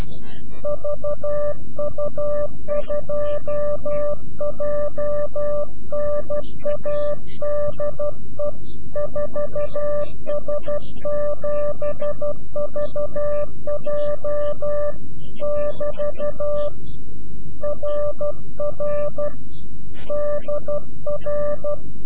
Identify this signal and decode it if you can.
Morse code, heard as a keyed tone, about 12 wpm, 590 Hz. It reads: VU2JNA DE 4S7VJ <BT> RR DR
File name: CW-QSO / CQ Call / CQ-QSO